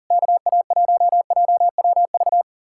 Morse code, heard as a keyed tone, about 53 wpm, spaced as for 46 wpm, 700 Hz. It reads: KA1JWV
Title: Speed-Racing Common QSO Phrases Encore - 40wpm